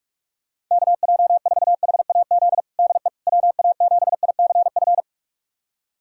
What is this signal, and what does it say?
Morse code, heard as a keyed tone, about 46 wpm, 700 Hz. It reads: KJ4HAZ DE WA7ICF
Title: USA Callsign DE Callsign - 6 Characters - 45wpm